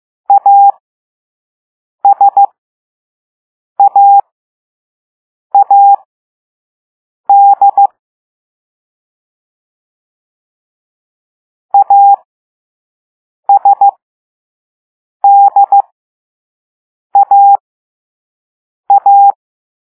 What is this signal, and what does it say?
Morse code, heard as a keyed tone, about 15 wpm, spaced as for 5 wpm, 800 Hz. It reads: ASAAD ASDAA
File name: Morse / Level3B